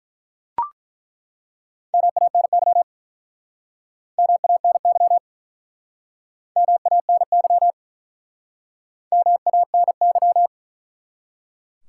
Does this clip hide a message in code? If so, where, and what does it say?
Yes: Morse code from 1.9 to 10.5 s, heard as a keyed tone, about 48 wpm, 700 Hz. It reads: MANY MANY MANY MANY